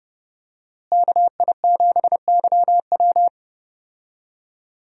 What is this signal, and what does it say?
Morse code, heard as a keyed tone, about 30 wpm, 700 Hz. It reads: KI7YW
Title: USA Call Signs - 5 Characters - Repeated 3x - 30wpm